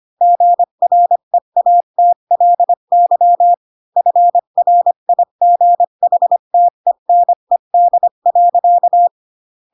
Morse code, heard as a keyed tone, about 25 wpm, 700 Hz. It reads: GREATLY FRIGHTENED.
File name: Pinocchio / Pinocchio ch4 20 25 ik1hge